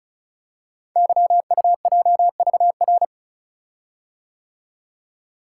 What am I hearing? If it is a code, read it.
Morse code, heard as a keyed tone, about 36 wpm, 700 Hz. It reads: YUJVR
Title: Code Groups - 5 Letters - 35wpm